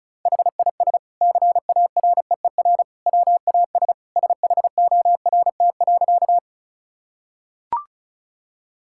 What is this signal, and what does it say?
Morse code, heard as a keyed tone, about 34 wpm, 700 Hz. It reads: HIS CAREER WAS SHORT.
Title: Sentences from Top 1000 Words - Review Part 6 - 35wpm